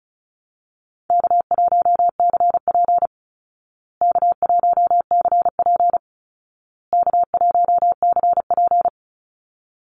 Morse code, heard as a keyed tone, about 34 wpm, 700 Hz. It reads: K1CP K1CP K1CP
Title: Easy Call Signs Repeated 3x - 35wpm